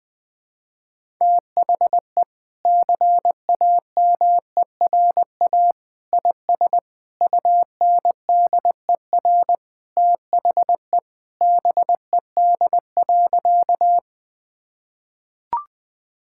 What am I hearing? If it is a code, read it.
Morse code, heard as a keyed tone, about 20 wpm, 700 Hz. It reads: THE CAMERA IS UNDER THE BED.
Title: Sentences from Top 1000 Words - Review Part 5 - 20wpm